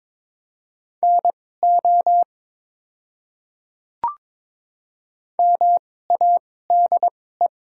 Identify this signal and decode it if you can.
Morse code, heard as a keyed tone, about 23 wpm, spaced as for 16 wpm, 700 Hz. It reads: NO MADE
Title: 2x Character Spacing - Top 300 Words - 22wpm